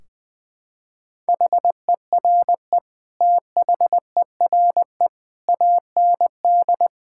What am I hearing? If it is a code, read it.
Morse code, heard as a keyed tone, about 20 wpm, 700 Hz. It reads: HERE THERE AND
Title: Sentences from Top 1600 Words 20wpm